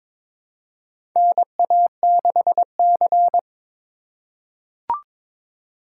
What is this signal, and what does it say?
Morse code, heard as a keyed tone, about 22 wpm, 700 Hz. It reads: NA6C
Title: USA Call Signs - 4 Characters - 3x Character Spacing - 22wpm